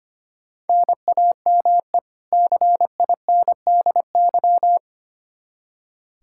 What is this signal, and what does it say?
Morse code, heard as a keyed tone, about 25 wpm, 700 Hz. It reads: NAME CINDY
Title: K1USN SST QSO Practice 25wpm